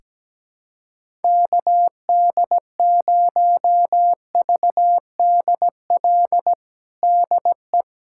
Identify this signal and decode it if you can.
Morse code, heard as a keyed tone, about 17 wpm, 700 Hz. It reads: KD0VDL DE
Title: USA Callsign DE Callsign - 6 Characters - 17wpm